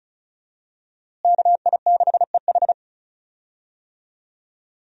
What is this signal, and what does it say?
Morse code, heard as a keyed tone, about 34 wpm, 700 Hz. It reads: KI6EH
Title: USA Call Signs - 5 Characters - Repeated 3x - 35wpm